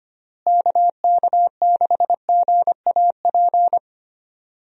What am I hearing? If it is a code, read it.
Morse code, heard as a keyed tone, about 25 wpm, 700 Hz. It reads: KK6GAP
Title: USA Call Signs - 6 Characters - Repeated 3x - 25wpm